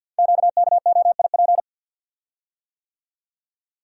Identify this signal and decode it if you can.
Morse code, heard as a keyed tone, about 48 wpm, 700 Hz. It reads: QKOIP
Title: Code Groups - 5 Letters - 50wpm